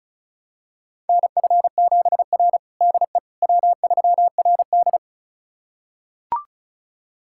Morse code, heard as a keyed tone, about 36 wpm, 700 Hz. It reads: NF7R DE W3RD
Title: USA Callsign DE Callsign - 4 Characters - 35wpm